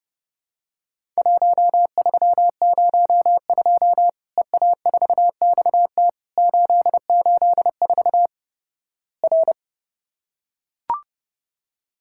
Morse code, heard as a keyed tone, about 30 wpm, 700 Hz. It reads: 1302 EA4XT 884 R
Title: Worked All Europe DX Contest - QTC Exchange - 30wpm